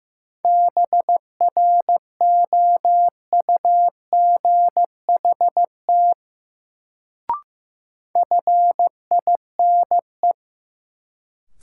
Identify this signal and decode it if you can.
Morse code, heard as a keyed tone, about 15 wpm, 700 Hz. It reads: BROUGHT FINE
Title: Sentences from Top 500 Words - New Word Drill - 15wpm